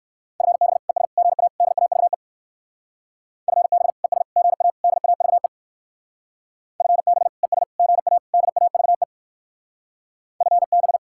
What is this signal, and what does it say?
Morse code, heard as a keyed tone, about 58 wpm, 700 Hz. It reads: FB ES GA DAVE FB ES GA DAVE FB ES GA DAVE FB